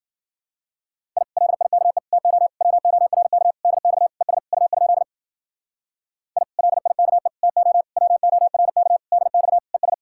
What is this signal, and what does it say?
Morse code, heard as a keyed tone, about 52 wpm, 700 Hz. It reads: I LIKE TO WORK DX ES U? I LIKE TO WORK DX ES